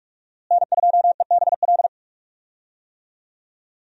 Morse code, heard as a keyed tone, about 46 wpm, 700 Hz. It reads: N2EBL
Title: USA Call Signs - 5 Characters - Repeated 2x - 45wpm